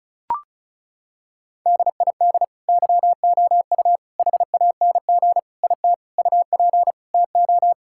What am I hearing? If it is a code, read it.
Morse code, heard as a keyed tone, about 36 wpm, 700 Hz. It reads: DID YOU HANG IT UP TO
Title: Sentences from Top 1600 Words 35wpm